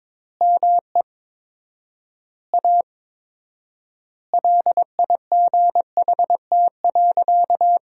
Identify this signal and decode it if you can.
Morse code, heard as a keyed tone, about 22 wpm, 700 Hz. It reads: ME A LIGHT.
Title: 4x Word-Spacing - Sentences from Top 200 Words - 22wpm